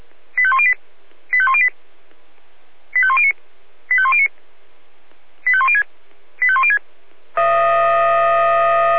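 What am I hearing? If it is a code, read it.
Five-tone selective call (ZVEI), heard as quick runs of five tones, tones of about 70 ms. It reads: 85108 85108 85109 85109 85106 85106
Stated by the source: ZVEI five-tone call 85300 received on an SDR.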